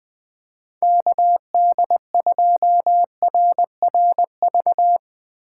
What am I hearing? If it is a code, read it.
Morse code, heard as a keyed tone, about 21 wpm, 700 Hz. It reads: KD2RRV
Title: USA Call Signs - 6 Characters - 4x Character Spacing - 20wpm